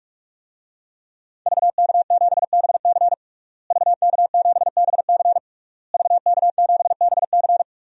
Morse code, heard as a keyed tone, about 41 wpm, 700 Hz. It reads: VK7BC VK7BC VK7BC
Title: DXCC-100 Call Signs - 5 Characters - Repeated 3x - 45wpm